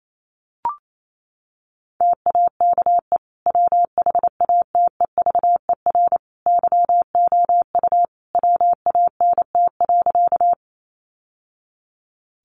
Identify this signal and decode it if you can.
Morse code, heard as a keyed tone, about 28 wpm, 700 Hz. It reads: TAKE WHATEVER YOU WANT.
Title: Sentences from Top 1100 Words 28wpm